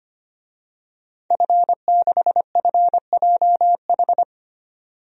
Morse code, heard as a keyed tone, about 25 wpm, 700 Hz. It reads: F6FJH
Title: DXCC-100 Call Signs - 5 Characters - 25wpm